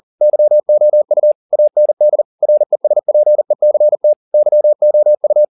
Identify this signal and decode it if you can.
Morse code, heard as a keyed tone, about 40 wpm, 600 Hz. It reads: YOU AND RESPECT YOU